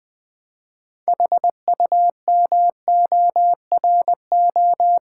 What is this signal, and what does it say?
Morse code, heard as a keyed tone, about 20 wpm, 700 Hz. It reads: HUMORO
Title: Build A Word - 2 Letters At A Time - 8-Letter Words - 20wpm